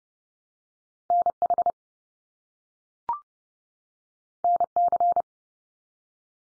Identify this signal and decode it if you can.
Morse code, heard as a keyed tone, about 30 wpm, 700 Hz. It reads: NH NC
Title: US States 30WPM